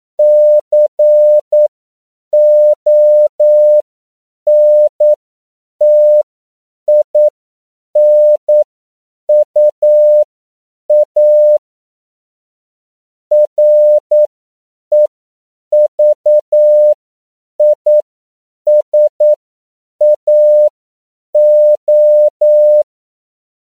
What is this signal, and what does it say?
Morse code, heard as a keyed tone, about 9 wpm, spaced as for 7 wpm, 600 Hz. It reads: CONTINUA REVISAO
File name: CW / 10PPM